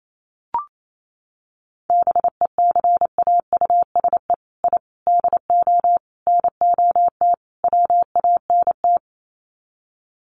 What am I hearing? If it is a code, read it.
Morse code, heard as a keyed tone, about 28 wpm, 700 Hz. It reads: BECAUSE I DO NOT WANT
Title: Sets of 5 Words Encore 28wpm